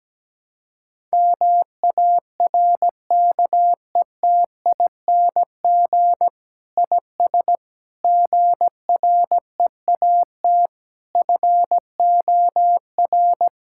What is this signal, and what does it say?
Morse code, heard as a keyed tone, about 17 wpm, 700 Hz. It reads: MARKETING IS GREAT FOR
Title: Sentences from Top 1000 Words - Review Part 5 - 17wpm